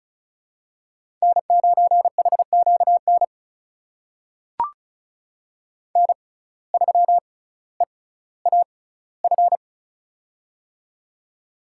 Morse code, heard as a keyed tone, about 36 wpm, spaced as for 18 wpm, 700 Hz. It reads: N9HQN N3EAF
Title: USA Call Signs - 6x Character Spacing - 35wpm